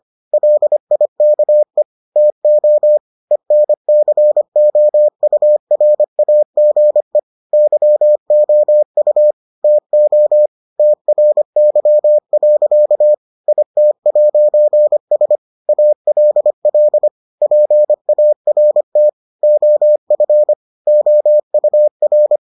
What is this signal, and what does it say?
Morse code, heard as a keyed tone, about 25 wpm, 600 Hz. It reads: LIKE TO ENCOURAGE YOU TO TRY. IT'S ALL PART OF OUR